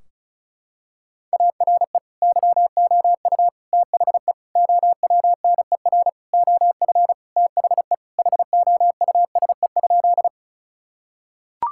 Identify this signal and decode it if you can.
Morse code, heard as a keyed tone, about 36 wpm, 700 Hz. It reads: ARE YOU THE OWNER OF THE HOUSE?
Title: Sentences from Top 1600 Words 35wpm